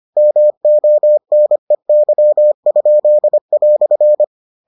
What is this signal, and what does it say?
Morse code, heard as a keyed tone, about 25 wpm, 600 Hz. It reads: MONEY?"